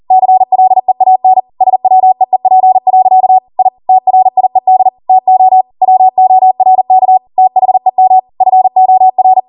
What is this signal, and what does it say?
Morse code, heard as a keyed tone, about 40 wpm, 750 Hz. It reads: CLEAN SWEEP. I TRIED TO WORK THEM FOR